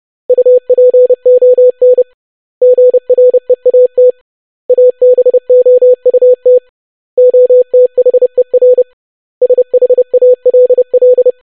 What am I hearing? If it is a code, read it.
Morse code, heard as a keyed tone, about 30 wpm, 500 Hz. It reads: UPON GREAT ABOUT OTHER SHALL
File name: basic100-25wpm